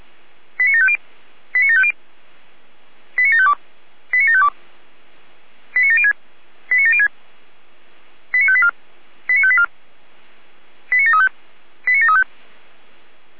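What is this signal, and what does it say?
Five-tone selective call (ZVEI), heard as quick runs of five tones, tones of about 70 ms. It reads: 89750 89750 89752 89752 89706 89706 89584 89584 89735 89735
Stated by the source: ZVEI five-tone call received by SDR, archived by Feuerwehr Orlamünde (fire brigade).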